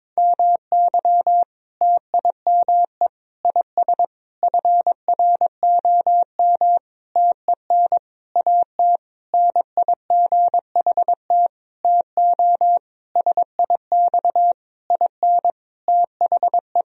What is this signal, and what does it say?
Morse code, heard as a keyed tone, about 22 wpm, 700 Hz. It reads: MY TIME IS FROM TEN AT NIGHT TO SIX IN THE
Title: Sentences from Top 1000 Words - Review Part 2 - 22wpm